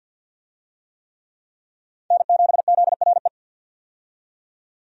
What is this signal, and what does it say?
Morse code, heard as a keyed tone, about 52 wpm, 700 Hz. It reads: N7ZRE
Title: USA Call Signs - 5 Characters - 50wpm